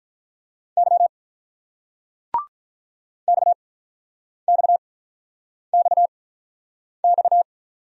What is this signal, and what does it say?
Morse code, heard as a keyed tone, about 44 wpm, spaced as for 9 wpm, 700 Hz. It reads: Y XXXX
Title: Speed-Racing Single Letters 35wpm